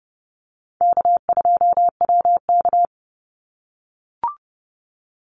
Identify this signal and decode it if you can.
Morse code, heard as a keyed tone, about 30 wpm, 700 Hz. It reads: K2WK
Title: Call Signs - 30wpm